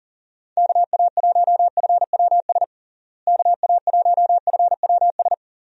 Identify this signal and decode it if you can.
Morse code, heard as a keyed tone, about 37 wpm, 700 Hz. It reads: KA1FWS KA1FWS
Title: USA Call Signs - 6 Characters - Repeated 2x - 40wpm